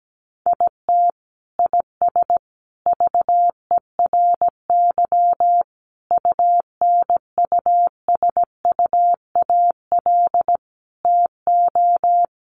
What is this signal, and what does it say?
Morse code, heard as a keyed tone, about 17 wpm, 700 Hz. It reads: IT IS VERY UNUSUAL TO
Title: Sentences from Top 1400 Words 17wpm